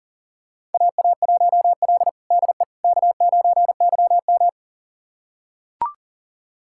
Morse code, heard as a keyed tone, about 40 wpm, 700 Hz. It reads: AA1L DE K9YM
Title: USA Callsign DE Callsign - 4 Characters - 40wpm